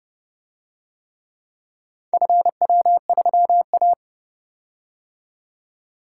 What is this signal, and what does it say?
Morse code, heard as a keyed tone, about 29 wpm, 700 Hz. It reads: FW3A